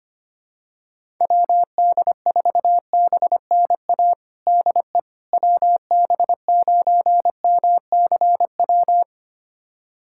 Morse code, heard as a keyed tone, about 25 wpm, 700 Hz. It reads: WD4BNA DE WB9MCW
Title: USA Callsign DE Callsign - 6 Characters - 25wpm